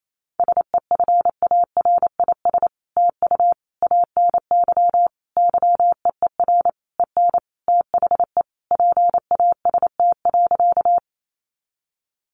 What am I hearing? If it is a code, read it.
Morse code, heard as a keyed tone, about 28 wpm, 700 Hz. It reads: SEFARIS TU ANY YEER EN THE PAST.
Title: A Sound Of Thunder 28wpm